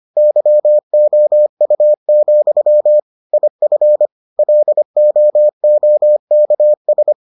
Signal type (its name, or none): Morse code